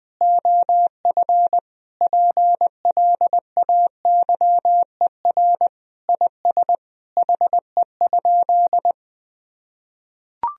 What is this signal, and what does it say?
Morse code, heard as a keyed tone, about 20 wpm, 700 Hz. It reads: OF PLAYER IS HE?
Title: Sentences from Top 1000 Words - Review Part 4 - 20wpm